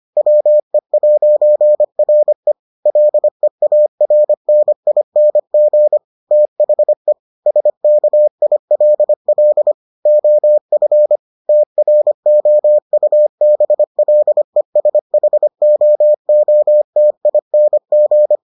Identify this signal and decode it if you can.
Morse code, heard as a keyed tone, about 25 wpm, 600 Hz. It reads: WE'RE LEARNING THE SKILL OF TROUBLESHOOTING